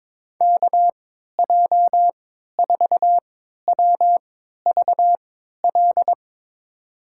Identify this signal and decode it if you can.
Morse code, heard as a keyed tone, about 22 wpm, spaced as for 13 wpm, 700 Hz. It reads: KJ4WVL
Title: USA Call Signs - 6 Characters - 3x Character Spacing - 22wpm